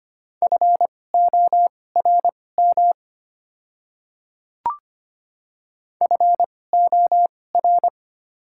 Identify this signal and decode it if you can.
Morse code, heard as a keyed tone, about 25 wpm, spaced as for 18 wpm, 690 Hz. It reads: FORM FOR
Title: 2x Character Spacing - Top 200 Words - 25wpm